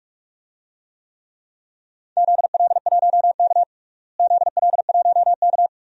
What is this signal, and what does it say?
Morse code, heard as a keyed tone, about 46 wpm, 700 Hz. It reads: ZL1K ZL1K
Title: DXCC-100 Call Signs - 4 Characters - Repeated 2x - 45wpm